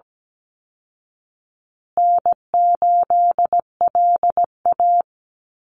Morse code, heard as a keyed tone, about 17 wpm, 700 Hz. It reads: N8LA
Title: Easy Call Signs Repeated 3x - 17wpm